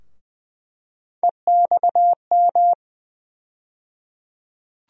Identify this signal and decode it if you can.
Morse code, heard as a keyed tone, about 20 wpm, 700 Hz. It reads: EXM
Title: USA Call Sign Postfixes - 3 Letters 20wpm